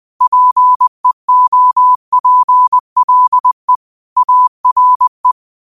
Morse code, heard as a keyed tone, about 20 wpm, 1000 Hz. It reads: PEOPLE ARE